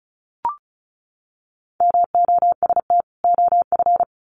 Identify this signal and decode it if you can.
Morse code, heard as a keyed tone, about 34 wpm, 700 Hz. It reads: MOST OF
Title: Sets of 2 Words 35wpm